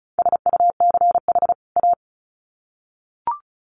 Morse code, heard as a keyed tone, about 34 wpm, 700 Hz. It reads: SUCH A
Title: Sets of 2 Words 35wpm